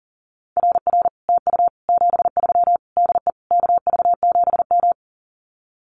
Morse code, heard as a keyed tone, about 38 wpm, 700 Hz. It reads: RR TU 73 DE KV7M
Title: Common QSO Phrases Encore 40wpm